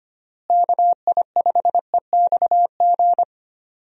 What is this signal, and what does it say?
Morse code, heard as a keyed tone, about 25 wpm, 700 Hz. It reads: KI5EXG
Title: USA Call Signs - 6 Characters - Repeated 2x - 25wpm